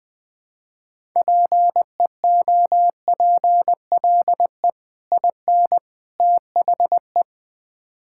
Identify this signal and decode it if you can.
Morse code, heard as a keyed tone, about 20 wpm, 700 Hz. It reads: PEOPLE IN THE
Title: Build A Phrase - 3-Word Phrases - 20wpm